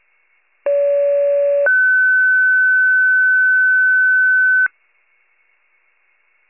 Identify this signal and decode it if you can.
Two-tone page, heard as a one-second tone, then a three-second tone, about 570 then 1500 Hz. It is heard from 0.7 to 4.7 s.
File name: pager / 29 FIRE / wcalert